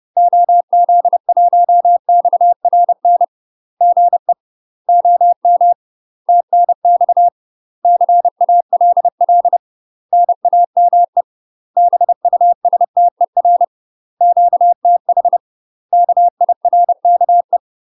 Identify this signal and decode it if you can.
Morse code, heard as a keyed tone, about 30 wpm, 700 Hz. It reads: OZ1XRN GE OM TNX CALL NAME BUSTER QTH KIRKE
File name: UK MASTER QSOs TRAINING / 30WPM / UK MASTER QSOs 0000